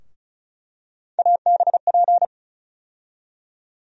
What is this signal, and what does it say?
Morse code, heard as a keyed tone, about 36 wpm, 700 Hz. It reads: ABP
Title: USA Call Sign Postfixes - 3 Letters 35wpm